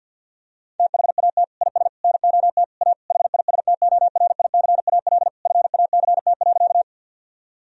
Morse code, heard as a keyed tone, about 50 wpm, 700 Hz. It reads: THAT IS NOT A HISTORICAL FACT.
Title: Sentences from Top 1000 Words - Review Part 4 - 50wpm